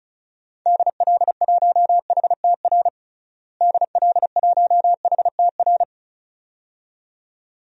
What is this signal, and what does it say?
Morse code, heard as a keyed tone, about 36 wpm, 700 Hz. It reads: DL1HTR DL1HTR
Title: DXCC-100 Call Signs - 6 Characters - Repeated 2x - 35wpm